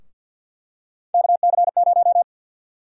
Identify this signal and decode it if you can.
Morse code, heard as a keyed tone, about 50 wpm, 700 Hz. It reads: KC0
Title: USA Call Sign Prefixes - 50wpm